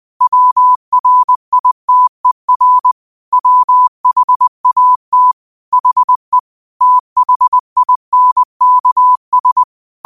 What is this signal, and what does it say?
Morse code, heard as a keyed tone, about 20 wpm, 1000 Hz. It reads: WRITER WHAT HE THINKS